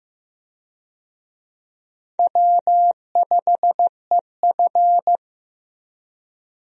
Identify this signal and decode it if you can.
Morse code, heard as a keyed tone, about 15 wpm, 700 Hz. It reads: W5EF
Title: USA Call Signs - 4 Characters - 15wpm